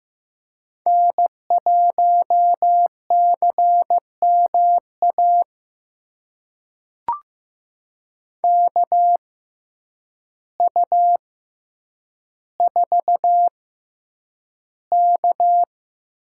Morse code, heard as a keyed tone, about 15 wpm, spaced as for 8 wpm, 700 Hz. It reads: N1CMA KU4K
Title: USA Call Signs - 6x Character Spacing - 15wpm